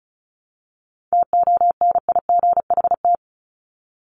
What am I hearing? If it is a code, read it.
Morse code, heard as a keyed tone, about 34 wpm, 700 Hz. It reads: TONIGHT
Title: Top 501-750 Words 35wpm